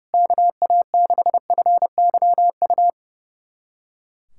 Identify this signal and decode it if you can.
Morse code, heard as a keyed tone, about 30 wpm, 700 Hz. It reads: KA6FYU
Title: USA Callsign DE Callsign - 6 Characters - 30wpm